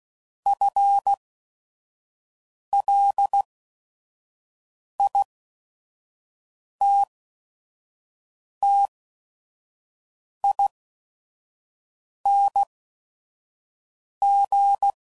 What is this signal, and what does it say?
Morse code, heard as a keyed tone, about 16 wpm, spaced as for 5 wpm, 800 Hz. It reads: FLITTING